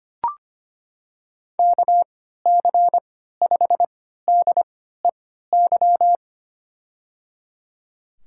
Morse code, heard as a keyed tone, about 25 wpm, spaced as for 14 wpm, 700 Hz. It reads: KC5DEY
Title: USA Call Signs - 6 Characters - 3x Character Spacing - 25wpm